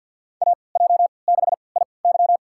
Morse code, heard as a keyed tone, about 53 wpm, spaced as for 29 wpm, 700 Hz. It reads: AJ6IY
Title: USA Call Signs - 5 Characters - 3x Character Spacing - 50wpm